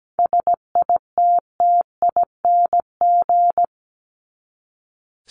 Morse code, heard as a keyed tone, about 17 wpm, 700 Hz. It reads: SITTING
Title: ING Suffix - Top 100 Words with Base Words 17wpm